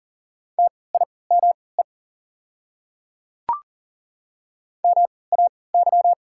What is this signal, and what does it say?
Morse code, heard as a keyed tone, about 40 wpm, spaced as for 23 wpm, 690 Hz. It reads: TIME MAY
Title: 3x Character Spacing - Top 100 Words - 40wpm